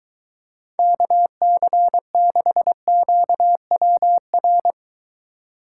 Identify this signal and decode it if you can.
Morse code, heard as a keyed tone, about 23 wpm, 700 Hz. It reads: KC6QWR